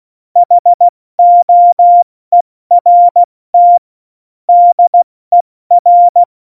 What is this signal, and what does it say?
Morse code, heard as a keyed tone, about 16 wpm, spaced as for 14 wpm, 700 Hz. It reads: HOERT DER
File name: Morsekurs / Witze / Witz 15. August 17, Tempo 70 F80